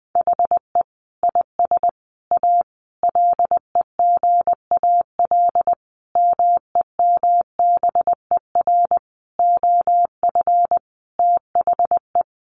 Morse code, heard as a keyed tone, about 20 wpm, 700 Hz. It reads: HE IS A LEGAL MEMBER OF THE
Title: Sentences from Top 1200 Words 20wpm